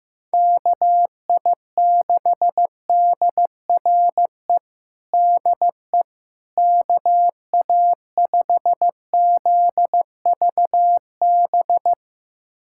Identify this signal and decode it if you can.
Morse code, heard as a keyed tone, about 15 wpm, 700 Hz. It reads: KI6DRE DE KA5ZVB
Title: USA Callsign DE Callsign 15wpm